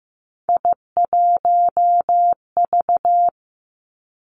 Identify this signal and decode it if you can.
Morse code, heard as a keyed tone, about 15 wpm, 700 Hz. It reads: I1V
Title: Three Challenging Letter-Number Combinations 15wpm